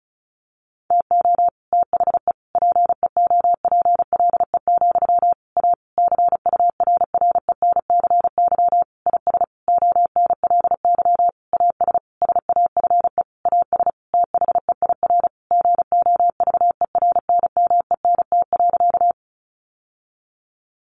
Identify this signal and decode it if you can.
Morse code, heard as a keyed tone, about 34 wpm, 700 Hz. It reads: TO THE PEOPLE, A CURRENCY IS ONLY AS SAFE AS THEIR GOVERNMENT.
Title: Sentences from Top 1500 Words 35wpm